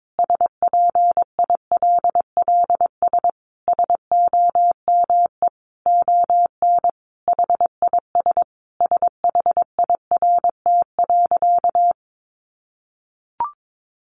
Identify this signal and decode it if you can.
Morse code, heard as a keyed tone, about 22 wpm, 700 Hz. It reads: SPILLS SOME ON HIS SHIRT.
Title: Additional Easy Phrases 22wpm